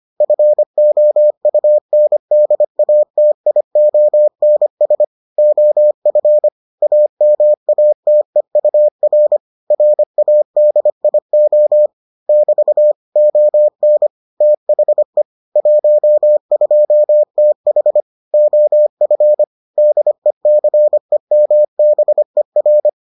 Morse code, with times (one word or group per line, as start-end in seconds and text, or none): 0.2-5.0 FOUNDATIONS
5.4-6.5 OF
6.8-9.4 AMATEUR
9.7-11.9 RADIO
12.3-12.9 <BT>
13.2-14.1 ON
14.4-15.2 THE
15.6-18.0 12TH
18.3-19.4 OF
19.8-22.9 DECEMBER